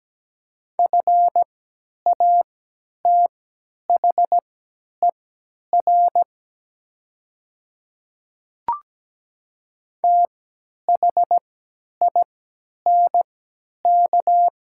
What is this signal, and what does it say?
Morse code, heard as a keyed tone, about 17 wpm, spaced as for 10 wpm, 700 Hz. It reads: FATHER THINK
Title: 3x Character Spacing - Top 300 Words - 17wpm